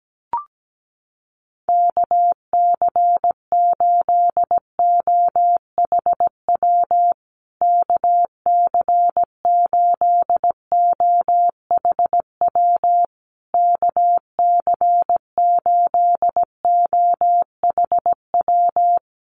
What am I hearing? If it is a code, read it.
Morse code, heard as a keyed tone, about 17 wpm, 700 Hz. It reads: KC8OHW KC8OHW KC8OHW
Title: USA Call Signs Repeated 3x 17wpm